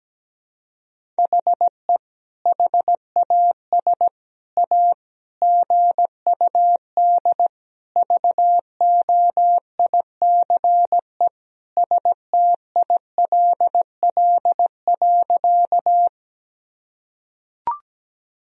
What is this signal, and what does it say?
Morse code, heard as a keyed tone, about 17 wpm, 700 Hz. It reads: HE HAS A GUD VOICE STILL.
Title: Sentences from Top 500 Words and Abbreviations 17wpm